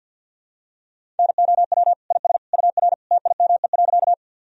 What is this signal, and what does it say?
Morse code, heard as a keyed tone, about 46 wpm, 700 Hz. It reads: NOW IS UR TIME.
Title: Sentences from Top 100 Words and Abbreviations 50wpm